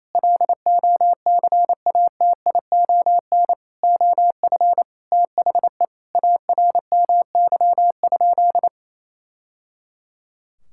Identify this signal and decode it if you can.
Morse code, heard as a keyed tone, about 28 wpm, 700 Hz. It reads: LOCATION OF THE ARMY?
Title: Sentences from Top 1000 Words - Review Part 6 - 28wpm